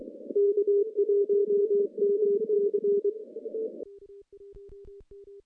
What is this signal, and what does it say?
Morse code, heard as a keyed tone, about 21 wpm, 400 Hz. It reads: K1MC
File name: contests / CW-SS-k5tr- -0100